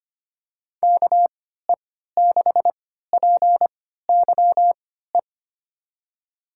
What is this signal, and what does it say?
Morse code, heard as a keyed tone, about 25 wpm, spaced as for 14 wpm, 700 Hz. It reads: KE6PYE
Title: USA Call Signs - 6 Characters - 3x Character Spacing - 25wpm